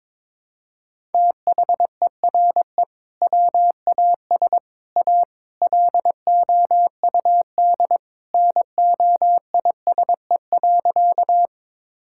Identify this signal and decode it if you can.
Morse code, heard as a keyed tone, about 22 wpm, 700 Hz. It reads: THERE WAS A LOUD NOISE.
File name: practice / 22wpm / Sentences with 3-Word Phrases - 22wpm